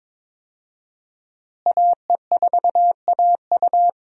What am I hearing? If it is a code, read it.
Morse code, heard as a keyed tone, about 22 wpm, 700 Hz. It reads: AE4AU
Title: USA Call Signs - 5 Characters - Repeated 2x - 22wpm